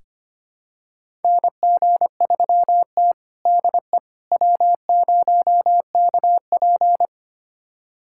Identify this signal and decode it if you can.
Morse code, heard as a keyed tone, about 25 wpm, 700 Hz. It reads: NG3T DE W0KP
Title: USA Callsign DE Callsign - 4 Characters - 25wpm